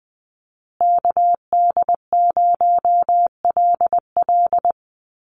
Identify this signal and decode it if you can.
Morse code, heard as a keyed tone, about 20 wpm, 700 Hz. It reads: KD0LL
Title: USA Call Signs Repeated 3x 20wpm